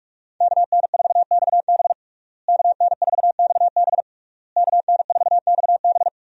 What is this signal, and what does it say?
Morse code, heard as a keyed tone, about 46 wpm, 700 Hz. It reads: KN4XB KN4XB KN4XB
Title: USA Call Signs - 5 Characters - Repeated 3x - 45wpm